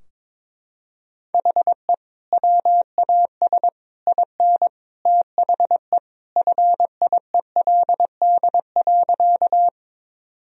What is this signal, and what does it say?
Morse code, heard as a keyed tone, about 22 wpm, 700 Hz. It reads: HE WAS IN THE FIELD.
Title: Sentences from Top 1000 Words - Review Part 2 - 22wpm